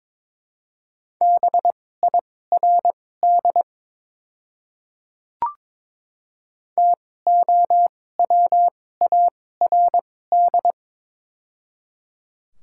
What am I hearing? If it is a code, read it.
Morse code, heard as a keyed tone, about 22 wpm, spaced as for 16 wpm, 700 Hz. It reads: BIRD TOWARD
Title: 2x Character Spacing - Top 500 Words - 22wpm